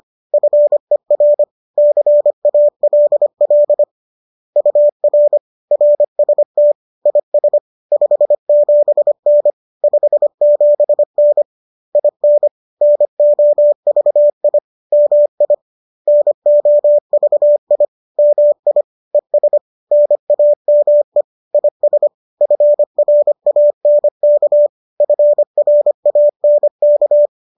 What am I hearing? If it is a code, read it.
Morse code, heard as a keyed tone, about 25 wpm, 590 Hz. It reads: FER CALL UR RST IS 57N 57N IN NOVI MI NOVI MI ES NAME IS FRANK FRANK
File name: qso221 25